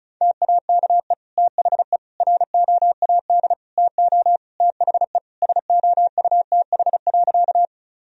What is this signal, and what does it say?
Morse code, heard as a keyed tone, about 34 wpm, 700 Hz. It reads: TAKE THE ROAD TO THE SOUTH.
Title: Sentences from Top 1000 Words - Review Part 2 - 35wpm